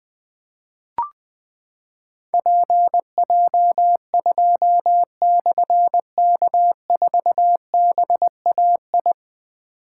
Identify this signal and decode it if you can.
Morse code, heard as a keyed tone, about 20 wpm, 700 Hz. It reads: PJ2/K4BAI
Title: Difficult Call Signs 20wpm